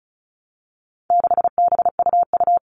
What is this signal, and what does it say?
Morse code, heard as a keyed tone, about 34 wpm, 700 Hz. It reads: 6BUU
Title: All Cut Numbers - 4-Digit 35wpm